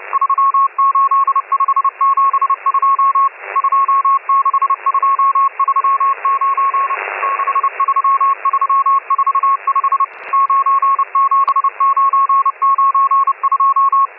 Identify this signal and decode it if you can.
Morse code, heard as a keyed tone, about 30 wpm, 1090 Hz, with radio noise. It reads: 38572 26239 63345 97992